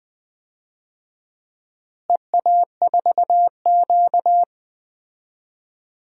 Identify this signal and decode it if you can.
Morse code, heard as a keyed tone, about 20 wpm, 700 Hz. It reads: EA4Q
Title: DXCC-100 Call Signs - 4 Characters - Repeated 2x - 20wpm